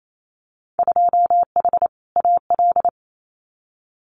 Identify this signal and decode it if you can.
Morse code, heard as a keyed tone, about 28 wpm, 700 Hz. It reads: 2H AL
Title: Winter Field Day Exchanges 28wpm